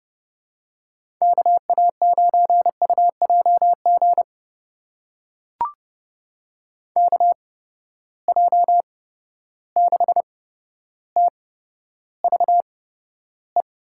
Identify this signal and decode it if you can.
Morse code, heard as a keyed tone, about 30 wpm, spaced as for 13 wpm, 700 Hz. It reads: KA9UJG KJ6TVE
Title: USA Call Signs - 6 Characters - 8x Character Spacing - 30wpm